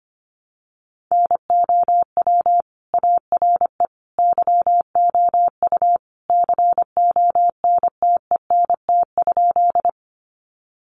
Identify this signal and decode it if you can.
Morse code, heard as a keyed tone, about 25 wpm, 700 Hz. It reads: NOW ARE YOU CONTENT?
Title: Sentences from Top 900 Words 25wpm